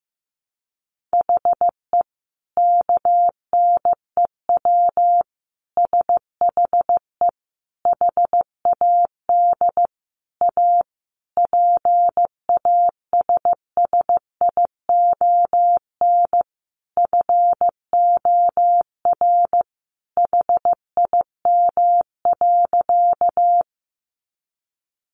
Morse code, heard as a keyed tone, about 15 wpm, 700 Hz. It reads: HE KNEW SHE HAD A PASSION FOR HIM.
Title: Sentences from Top 1300 Words 15wpm